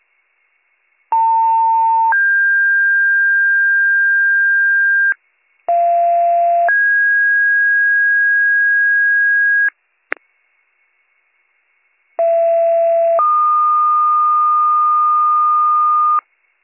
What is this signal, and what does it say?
Two-tone page, heard as a one-second tone, then a three-second tone, about 900 then 1600 Hz, about 690 then 1650 Hz, about 670 then 1220 Hz, with noise. It is heard from 1.1 to 16.2 s.